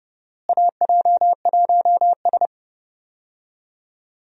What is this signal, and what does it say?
Morse code, heard as a keyed tone, about 32 wpm, 700 Hz. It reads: AJ1S
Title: USA Call Signs - 4 Characters - Repeated 2x - 30wpm